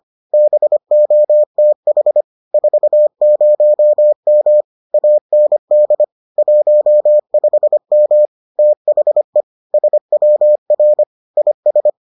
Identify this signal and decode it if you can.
Morse code, heard as a keyed tone, about 25 wpm, 600 Hz. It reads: BOTH 40M AND 15M THE SWR IS